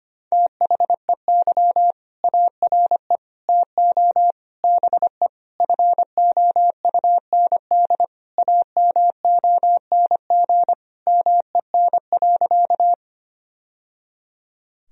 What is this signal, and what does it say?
Morse code, heard as a keyed tone, about 25 wpm, 700 Hz. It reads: THEY ARE TO BE FOUND AMONG MEN.
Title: Sentences from Top 1000 Words - Review Part 3 - 25wpm